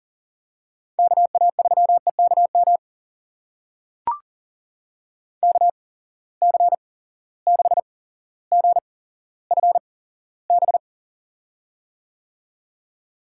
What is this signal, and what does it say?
Morse code, heard as a keyed tone, about 40 wpm, spaced as for 17 wpm, 700 Hz. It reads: KA3EKM KC6GFB
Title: USA Call Signs - 6 Characters - 8x Character Spacing - 40wpm